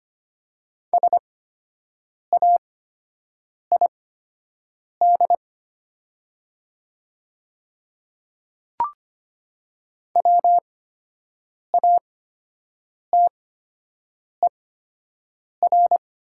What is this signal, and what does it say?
Morse code, heard as a keyed tone, about 25 wpm, spaced as for 7 wpm, 700 Hz. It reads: SAID WATER
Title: 8x Character Spacing - Top 100 Words - 25wpm